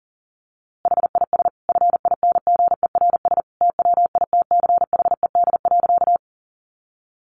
Morse code, heard as a keyed tone, about 40 wpm, 700 Hz. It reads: HIS FINGERS TWITCHED.
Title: A Sound Of Thunder 40wpm